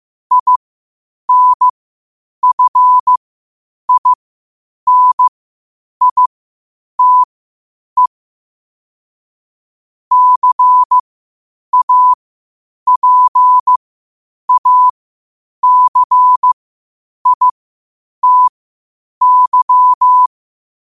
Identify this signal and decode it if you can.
Morse code, heard as a keyed tone, about 15 wpm, spaced as for 8 wpm, 1000 Hz. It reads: INFINITE CAPACITY
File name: qotdfiles / 8wpm